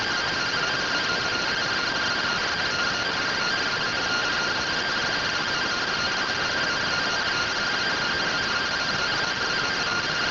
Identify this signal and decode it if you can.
Two-tone teleprinter signal, about 170 Hz shift.